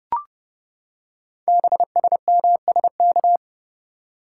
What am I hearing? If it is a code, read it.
Morse code, heard as a keyed tone, about 30 wpm, 700 Hz. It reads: BSMSK
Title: Code Groups - 5 Letters - 30wpm